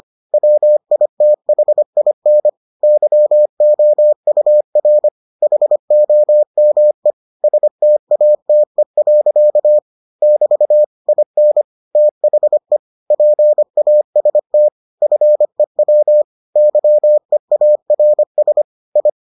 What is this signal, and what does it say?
Morse code, heard as a keyed tone, about 25 wpm, 600 Hz. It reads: WITHIN YOUR HOME STATE. <BT> IN THE PAST FEW YEARS I